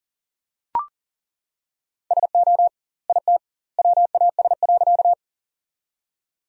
Morse code, heard as a keyed tone, about 40 wpm, 700 Hz. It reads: SO IT WAS.
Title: 2x Word-Spacing - Sentences from Top 100 Words - 40wpm